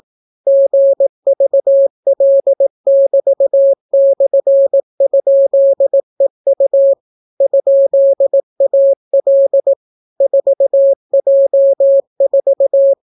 Morse code, heard as a keyed tone, about 18 wpm, 550 Hz. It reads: GVL<BT>/?EU ?AL 4J4